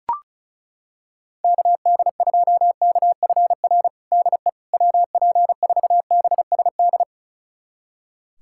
Morse code, heard as a keyed tone, about 36 wpm, 700 Hz. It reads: KD2KFR DE WP4BSD
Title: USA Callsign DE Callsign 35wpm